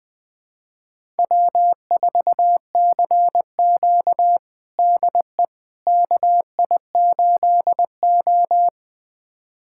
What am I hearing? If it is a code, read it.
Morse code, heard as a keyed tone, about 20 wpm, 700 Hz. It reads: W4CQ DE KI8O
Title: USA Callsign DE Callsign - 4 Characters - 20wpm